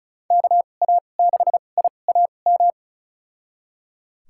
Morse code, heard as a keyed tone, about 34 wpm, spaced as for 25 wpm, 700 Hz. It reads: KA6IAM
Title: USA Call Signs - 2x Character Spacing - 35wpm